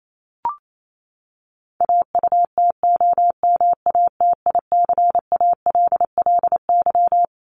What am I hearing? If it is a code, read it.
Morse code, heard as a keyed tone, about 28 wpm, 700 Hz. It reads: AUTOMATICALLY
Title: Top 1751-2000 Words 28wpm